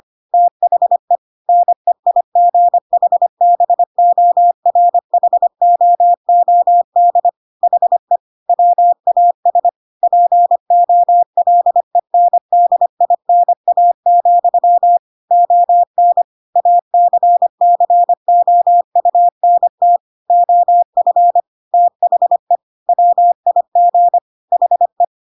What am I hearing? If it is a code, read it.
Morse code, heard as a keyed tone, about 25 wpm, 700 Hz. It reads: THE NEIGHBORHOOD HE WAS POLENDINA, ON ACCOUNT OF THE WIG HE